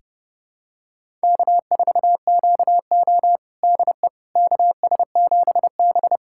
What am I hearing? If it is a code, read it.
Morse code, heard as a keyed tone, about 30 wpm, 700 Hz. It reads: K4QO DE KS7B
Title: USA Callsign DE Callsign - 4 Characters - 30wpm